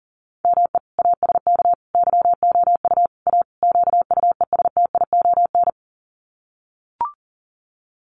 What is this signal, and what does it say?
Morse code, heard as a keyed tone, about 37 wpm, 700 Hz. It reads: ME ASK YOU A QUESTION
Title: Sets of 5 Words Encore2 40wpm